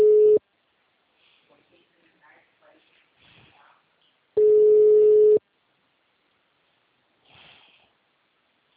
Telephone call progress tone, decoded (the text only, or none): ringback tone